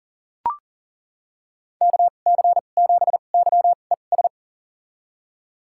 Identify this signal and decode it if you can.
Morse code, heard as a keyed tone, about 40 wpm, spaced as for 29 wpm, 700 Hz. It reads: KC7YES
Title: USA Call Signs - 2x Character Spacing - 40wpm